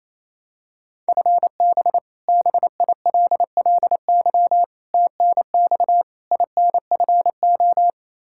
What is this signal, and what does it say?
Morse code, heard as a keyed tone, about 28 wpm, 700 Hz. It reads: FB BILLY TNX INFO
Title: K1USN SST QSO Practice 28wpm